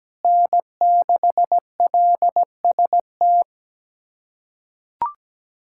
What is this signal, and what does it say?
Morse code, heard as a keyed tone, about 17 wpm, 700 Hz. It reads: N6LST
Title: USA Callsign DE Callsign 17wpm